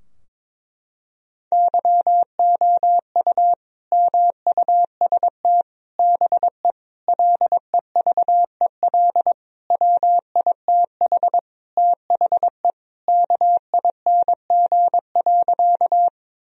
Morse code, heard as a keyed tone, about 22 wpm, 700 Hz. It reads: YOU MUST BE LEVEL WITH THE KING.
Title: Sentences from Top 1000 Words - Review Part 4 - 22wpm